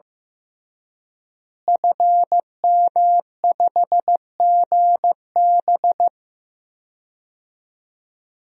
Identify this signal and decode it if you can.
Morse code, heard as a keyed tone, about 15 wpm, 700 Hz. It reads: FM5GB
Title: DXCC-100 Call Signs - 5 Characters - Repeated 3x - 15wpm